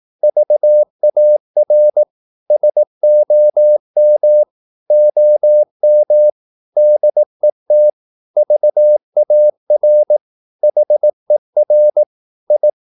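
Morse code, heard as a keyed tone, about 18 wpm, 600 Hz. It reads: VAR SOM OM DET VAR HER I
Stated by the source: Slow (QRS) Morse practice at 18 wpm, CW med Gnister.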